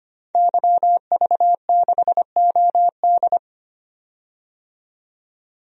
Morse code, heard as a keyed tone, about 25 wpm, 700 Hz. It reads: YV6OD
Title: DXCC-100 Call Signs - 5 Characters - 25wpm